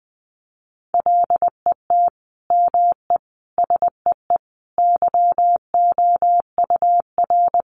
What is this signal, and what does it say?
Morse code, heard as a keyed tone, about 20 wpm, 700 Hz. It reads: LET ME SEE YOUR
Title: Sentences from Top 500 Words 20wpm